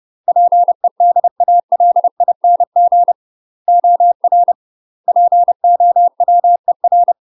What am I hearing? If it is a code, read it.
Morse code, heard as a keyed tone, about 30 wpm, 700 Hz. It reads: PEDALING OR POWER
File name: UK MASTER QSOs TRAINING / 30WPM / UK MASTER QSOs 0019